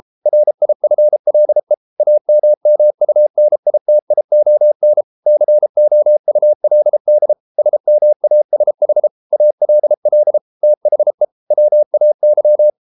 Morse code, heard as a keyed tone, about 33 wpm, 600 Hz. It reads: RIFLE AMMUNITION COULD SMASH ALL THE WAY